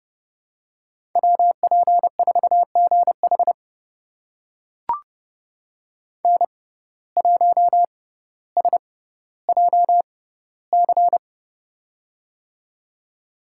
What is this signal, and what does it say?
Morse code, heard as a keyed tone, about 30 wpm, spaced as for 15 wpm, 700 Hz. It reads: WP4GH N1SJC
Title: USA Call Signs - 5 Characters - 6x Character Spacing - 30wpm